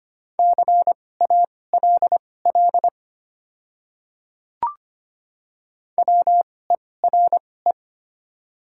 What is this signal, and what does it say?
Morse code, heard as a keyed tone, about 25 wpm, spaced as for 18 wpm, 700 Hz. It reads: CALL WERE
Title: 2x Character Spacing - Top 500 Words - 25wpm